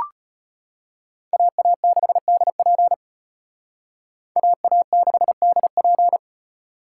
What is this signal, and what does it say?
Morse code, heard as a keyed tone, about 34 wpm, 700 Hz. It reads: AA6DP AA6DP